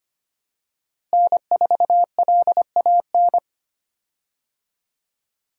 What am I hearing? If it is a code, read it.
Morse code, heard as a keyed tone, about 25 wpm, 690 Hz. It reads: N4LAN
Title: USA Call Signs - 5 Characters - Repeated 2x - 25wpm